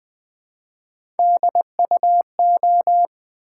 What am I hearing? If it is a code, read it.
Morse code, heard as a keyed tone, about 20 wpm, 700 Hz. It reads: DUO
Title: Build A Word - 1 Letter At A Time - 3-Letter Words - 20wpm